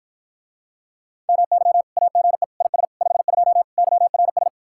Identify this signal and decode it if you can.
Morse code, heard as a keyed tone, about 54 wpm, 700 Hz. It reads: MY AGE IS 52 YRS